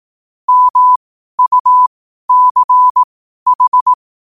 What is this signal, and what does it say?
Morse code, heard as a keyed tone, about 18 wpm, spaced as for 12 wpm, 1000 Hz. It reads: MUCH
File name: qotdfiles / 12wpm